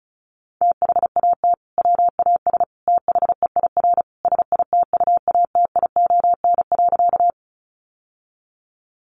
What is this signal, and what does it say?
Morse code, heard as a keyed tone, about 34 wpm, 700 Hz. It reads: THAT WAS THEIR SITUATION.
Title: Sentences from Top 1200 Words 35wpm